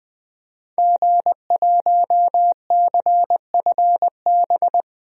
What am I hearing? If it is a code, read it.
Morse code, heard as a keyed tone, about 20 wpm, 700 Hz. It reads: G1CFB
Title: DXCC-100 Call Signs - 5 Characters - Repeated 2x - 20wpm